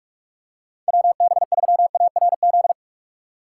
Morse code, heard as a keyed tone, about 43 wpm, 700 Hz. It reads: WB3ARZ
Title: USA Call Signs - 6 Characters - 6x Character Spacing - 45wpm